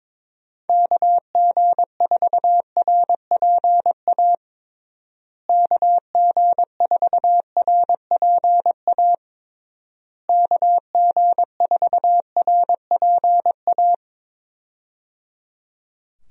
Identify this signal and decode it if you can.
Morse code, heard as a keyed tone, about 22 wpm, 700 Hz. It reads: KG4RPA KG4RPA KG4RPA
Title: USA Call Signs - 6 Characters - Repeated 3x - 22wpm